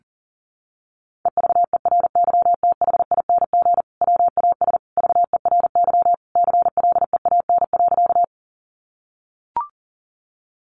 Morse code, heard as a keyed tone, about 40 wpm, 700 Hz. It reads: EVERYTHING WAS VERY CLEAN.
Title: Sentences from Top 1100 Words 40wpm